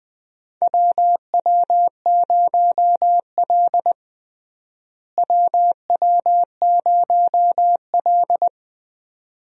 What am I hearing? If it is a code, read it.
Morse code, heard as a keyed tone, about 20 wpm, 700 Hz. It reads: WW0L WW0L
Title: USA Call Signs - 4 Characters - Repeated 2x - 20wpm